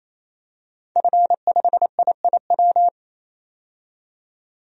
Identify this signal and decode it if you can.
Morse code, heard as a keyed tone, about 28 wpm, 700 Hz. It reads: F5IIW
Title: DXCC-100 Call Signs - 5 Characters - 28wpm